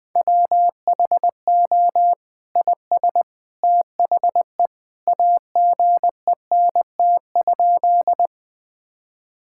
Morse code, heard as a keyed tone, about 19 wpm, 700 Hz. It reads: WHO IS THE AGENT?
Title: Sentences from Top 1600 Words 20wpm